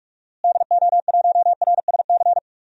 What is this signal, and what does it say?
Morse code, heard as a keyed tone, about 42 wpm, 700 Hz. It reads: DO1RSC
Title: DXCC-100 Call Signs - 6 Characters - Repeated 2x - 45wpm